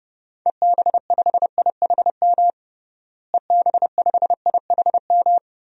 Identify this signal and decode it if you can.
Morse code, heard as a keyed tone, about 30 wpm, 700 Hz. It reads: EB5IHM EB5IHM
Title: DXCC-100 Call Signs - 6 Characters - Repeated 3x - 30wpm